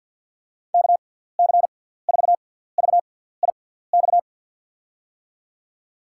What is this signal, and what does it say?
Morse code, heard as a keyed tone, about 51 wpm, spaced as for 17 wpm, 700 Hz. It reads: KC4VIX
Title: USA Call Signs - 6 Characters - 6x Character Spacing - 50wpm